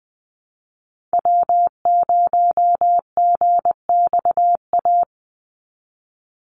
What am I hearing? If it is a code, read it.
Morse code, heard as a keyed tone, about 20 wpm, 700 Hz. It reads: W0GXA
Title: Call Signs Repeated 2x - 20wpm